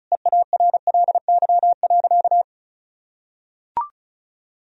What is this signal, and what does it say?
Morse code, heard as a keyed tone, about 36 wpm, 700 Hz. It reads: EARLY.
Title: Sentences from Top 1000 Words - Review Part 2 - 35wpm